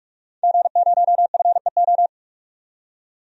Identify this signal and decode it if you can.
Morse code, heard as a keyed tone, about 46 wpm, 700 Hz. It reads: G0FEO
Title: DXCC-100 Call Signs - 5 Characters - 45wpm